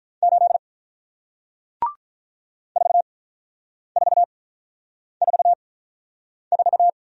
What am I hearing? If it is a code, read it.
Morse code, heard as a keyed tone, about 47 wpm, spaced as for 9 wpm, 700 Hz. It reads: 8 4444